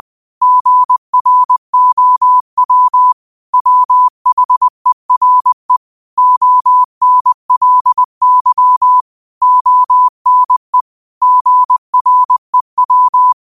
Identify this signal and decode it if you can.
Morse code, heard as a keyed tone, about 20 wpm, 1000 Hz. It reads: GROW WHERE ONLY ONE GREW
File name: qotdfiles / 20wpm